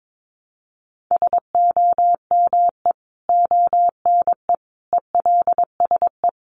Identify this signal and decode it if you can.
Morse code, heard as a keyed tone, about 22 wpm, 700 Hz. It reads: SOME ONE ELSE
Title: Sentences from Top 1100 Words 22wpm